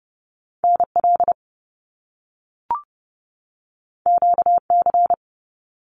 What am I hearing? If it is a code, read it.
Morse code, heard as a keyed tone, about 30 wpm, 700 Hz. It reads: NL QC
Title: Canadian Provinces and Territories 30wpm